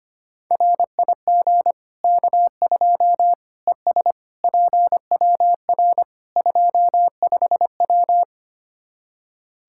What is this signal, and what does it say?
Morse code, heard as a keyed tone, about 25 wpm, 700 Hz. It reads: RIG K2 ES PWR 25W
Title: Practice QSO - Exchange 2 - Info - 25wpm